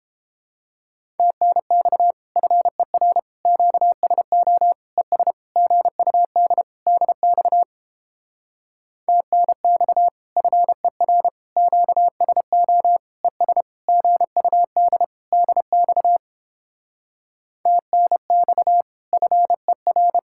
Morse code, heard as a keyed tone, about 31 wpm, 700 Hz. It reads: TNX FER QSO ES GUD DX TNX FER QSO ES GUD DX TNX FER